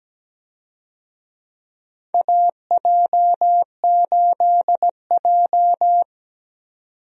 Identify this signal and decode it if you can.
Morse code, heard as a keyed tone, about 17 wpm, 700 Hz. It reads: AJ8J
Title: USA Call Signs - 4 Characters - 17wpm